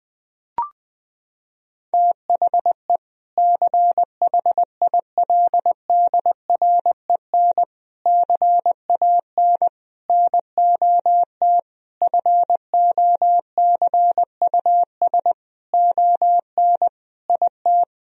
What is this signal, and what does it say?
Morse code, heard as a keyed tone, about 20 wpm, 700 Hz. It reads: THE CHILDREN CAN NOT FOCUS ON IT
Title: Sentences from Top 1000 Words - Review Part 4 - 20wpm